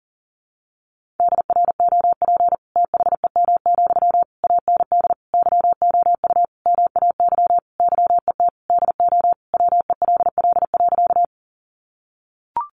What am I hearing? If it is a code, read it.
Morse code, heard as a keyed tone, about 40 wpm, 700 Hz. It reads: DROP THEM, AND YOU MAY YET DO WELL.
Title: Sentences from Top 1400 Words 40wpm